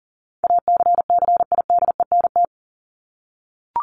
Morse code, heard as a keyed tone, about 40 wpm, 700 Hz. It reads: ACCIDENT
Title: Top 1751-2000 Words 40wpm